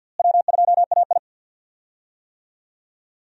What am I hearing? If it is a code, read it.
Morse code, heard as a keyed tone, about 49 wpm, 700 Hz. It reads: W2AI
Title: USA Call Signs - 4 Characters - Repeated 3x - 50wpm